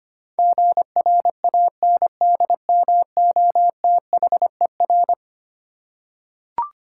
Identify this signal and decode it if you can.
Morse code, heard as a keyed tone, about 25 wpm, 700 Hz. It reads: GRANDMOTHER
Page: morse-code-ninja-practice-25wpm